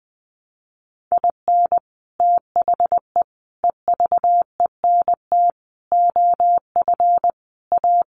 Morse code, heard as a keyed tone, about 20 wpm, 700 Hz. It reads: IN THE EVENT OF A
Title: Sets of 5 Words Encore 20wpm